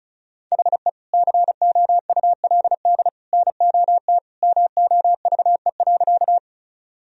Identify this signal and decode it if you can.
Morse code, heard as a keyed tone, about 34 wpm, 700 Hz. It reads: HE COULD NOT MOVE.
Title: Sentences from Top 1000 Words - Review Part 1 - 35wpm